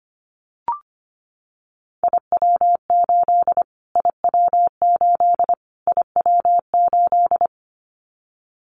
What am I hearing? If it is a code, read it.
Morse code, heard as a keyed tone, about 25 wpm, 700 Hz. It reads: IW8 IW8 IW8
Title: Callsign Prefixes for Top 25 DXCC - Repeated 3x - 25wpm